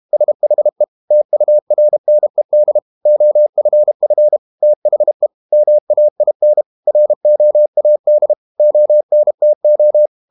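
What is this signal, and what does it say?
Morse code, heard as a keyed tone, about 32 wpm, 600 Hz. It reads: SHE TURNED OFF THE MAIN ROAD ONTO
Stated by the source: Morse code practice at 32 wpm from CW med Gnister, reading Ken Follett's novel Never in English.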